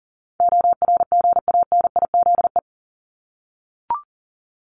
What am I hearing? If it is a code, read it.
Morse code, heard as a keyed tone, about 40 wpm, 700 Hz. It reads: ORGANIZE
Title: Top 1751-2000 Words 40wpm